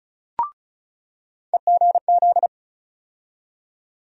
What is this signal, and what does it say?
Morse code, heard as a keyed tone, about 35 wpm, 700 Hz. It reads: EGZ
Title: USA Call Sign Postfixes - 3 Letters 35wpm